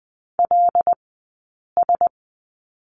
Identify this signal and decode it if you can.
Morse code, heard as a keyed tone, about 20 wpm, 700 Hz. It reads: L S
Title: 106 Lesson 18d - Letter m - Two Characters 20wpm